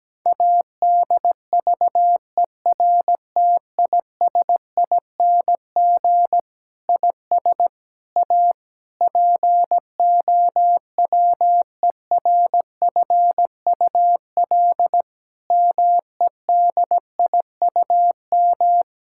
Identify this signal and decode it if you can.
Morse code, heard as a keyed tone, about 17 wpm, 700 Hz. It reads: ADVERTISING IS A POWERFUL MEDIUM
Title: Sentences from Top 1000 Words - Review Part 6 - 17wpm